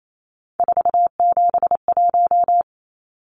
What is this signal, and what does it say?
Morse code, heard as a keyed tone, about 28 wpm, 700 Hz. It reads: 471
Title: Numbers - 3-Digit 28wpm